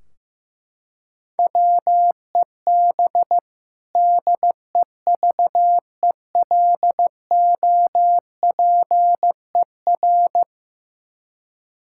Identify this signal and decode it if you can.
Morse code, heard as a keyed tone, about 15 wpm, 700 Hz. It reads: WEB DEVELOPER
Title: Practice QSO - Exchange 3 - More Info - 15wpm